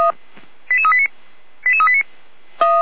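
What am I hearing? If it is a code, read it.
Five-tone selective call (ZVEI), heard as quick runs of five tones, tones of about 70 ms. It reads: 88389 88389
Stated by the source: ZVEI five-tone selective call received by SDR.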